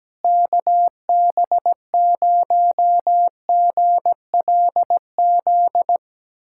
Morse code, heard as a keyed tone, about 17 wpm, 700 Hz. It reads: KB0GLZ
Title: USA Callsign DE Callsign - 6 Characters - 17wpm